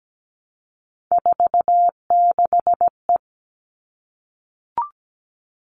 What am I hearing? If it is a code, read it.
Morse code, heard as a keyed tone, about 17 wpm, 700 Hz. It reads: 46E
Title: All Cut Numbers - 3-Digit 17wpm